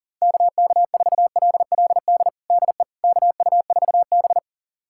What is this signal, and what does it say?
Morse code, heard as a keyed tone, about 40 wpm, 700 Hz. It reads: KK4LLD DE KU4B
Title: USA Callsign DE Callsign 40wpm